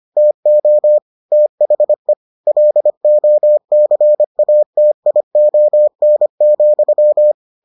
Morse code, heard as a keyed tone, about 25 wpm, 600 Hz. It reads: TO THE LOCATION,